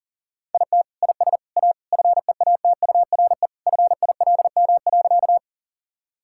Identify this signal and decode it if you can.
Morse code, heard as a keyed tone, about 40 wpm, 700 Hz. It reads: IT IS A FEATURE FILM.
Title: Sentences from Top 1000 Words - Review Part 5 - 40wpm